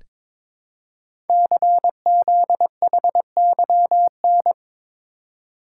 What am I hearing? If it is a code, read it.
Morse code, heard as a keyed tone, about 22 wpm, 700 Hz. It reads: CZHYN
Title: Code Groups - 5 Letters - 22wpm